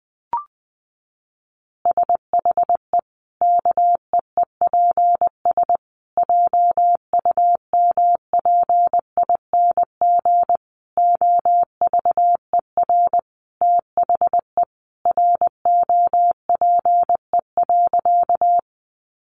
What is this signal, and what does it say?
Morse code, heard as a keyed tone, about 20 wpm, 700 Hz. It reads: SHE KEEPS JUMPING OVER THE ROPE.
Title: Easy Phrases 20wpm